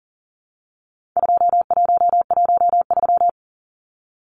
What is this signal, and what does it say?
Morse code, heard as a keyed tone, about 40 wpm, 700 Hz. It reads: 2113
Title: Numbers - 4-Digit 40wpm